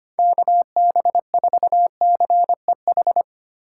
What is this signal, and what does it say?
Morse code, heard as a keyed tone, about 25 wpm, 700 Hz. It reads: KB4CEH
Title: USA Call Signs - 6 Characters - 4x Character Spacing - 25wpm